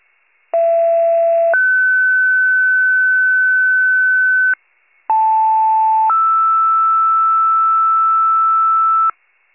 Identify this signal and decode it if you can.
Two-tone page, heard as a one-second tone, then a three-second tone, about 670 then 1500 Hz, about 880 then 1320 Hz, with noise. It is heard from 0.5 to 9.1 s.